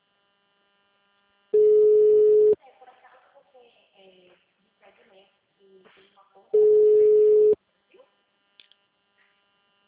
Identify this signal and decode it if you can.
Ringback tone (Europe, 425 Hz), with speech.